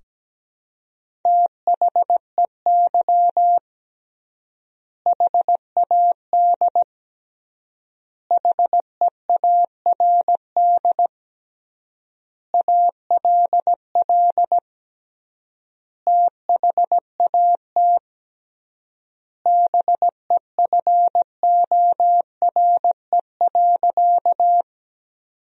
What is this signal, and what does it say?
Morse code, heard as a keyed tone, about 17 wpm, 700 Hz. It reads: THEY HAD HEARD ALL THAT BEFORE.
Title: 3x Word-Spacing - Sentences from Top 400 Words - 17wpm